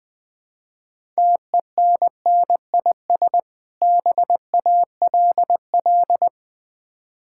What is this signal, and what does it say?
Morse code, heard as a keyed tone, about 20 wpm, 690 Hz. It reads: TENNIS BALL
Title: Difficult Words with Hints - 20wpm